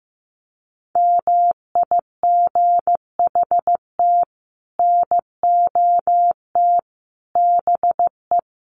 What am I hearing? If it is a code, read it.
Morse code, heard as a keyed tone, about 15 wpm, 700 Hz. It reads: MIGHT NOT BE
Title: Sets of 5 Words Encore2 15wpm